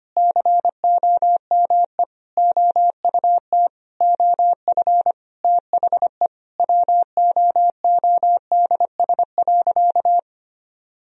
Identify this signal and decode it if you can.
Morse code, heard as a keyed tone, about 25 wpm, 700 Hz. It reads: COME OUT OF THE WOODS.
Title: Sentences from Top 1000 Words - Review Part 2 - 25wpm